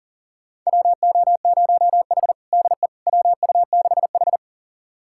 Morse code, heard as a keyed tone, about 40 wpm, 700 Hz. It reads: WO0H DE WU6H
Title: USA Callsign DE Callsign - 4 Characters - 40wpm